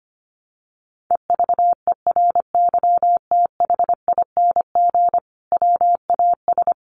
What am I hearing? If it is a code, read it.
Morse code, heard as a keyed tone, about 25 wpm, 700 Hz. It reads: EVERYTHING WAS
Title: Sentences from Top 700 Words 25wpm